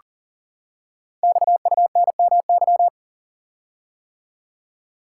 Morse code, heard as a keyed tone, about 42 wpm, 700 Hz. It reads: XUNMY
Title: Code Groups - 5 Letters - 40wpm